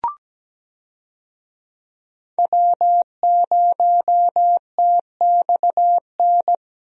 Morse code, heard as a keyed tone, about 17 wpm, 700 Hz. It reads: W0TXN
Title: USA Call Signs - 5 Characters - Repeated 3x - 17wpm